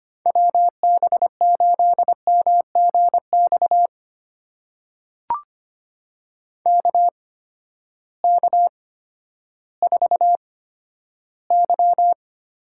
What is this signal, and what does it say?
Morse code, heard as a keyed tone, about 25 wpm, spaced as for 12 wpm, 700 Hz. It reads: WB8MGX KK4Y
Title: USA Call Signs - 6 Characters - 8x Character Spacing - 25wpm